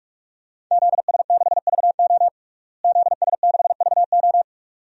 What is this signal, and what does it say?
Morse code, heard as a keyed tone, about 41 wpm, 700 Hz. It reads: ZS6VO ZS6VO
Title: DXCC-100 Call Signs - 5 Characters - Repeated 2x - 45wpm